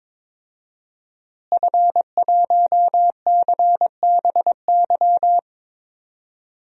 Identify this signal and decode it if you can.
Morse code, heard as a keyed tone, about 22 wpm, 700 Hz. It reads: F1CBY
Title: DXCC-100 Call Signs - 5 Characters - Repeated 3x - 22wpm